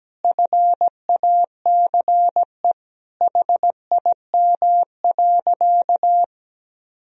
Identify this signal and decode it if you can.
Morse code, heard as a keyed tone, about 17 wpm, 700 Hz. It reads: FACE HIM.
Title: Sentences from Top 1000 Words - Review Part 2 - 17wpm